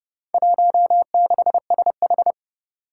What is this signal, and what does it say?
Morse code, heard as a keyed tone, about 30 wpm, 700 Hz. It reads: 16SH
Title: Four Challenging Letter-Number Combinations 30wpm